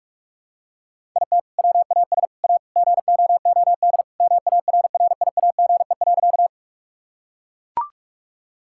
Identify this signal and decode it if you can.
Morse code, heard as a keyed tone, about 41 wpm, 700 Hz. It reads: IT WAS A GOOD MARRIAGE.
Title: Sentences from Top 1000 Words - Review Part 5 - 45wpm